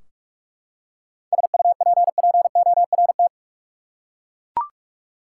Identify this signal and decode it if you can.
Morse code, heard as a keyed tone, about 42 wpm, 700 Hz. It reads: SUPPORT
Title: Practice QSO - Exchange 3 - More Info - 45wpm